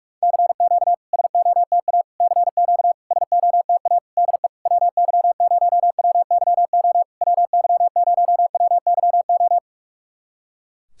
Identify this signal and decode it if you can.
Morse code, heard as a keyed tone, about 46 wpm, 700 Hz. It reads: CQ SOTA CQ SOTA DE WY0WYO WY0WYO
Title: SOTA QSO Practice 45wpm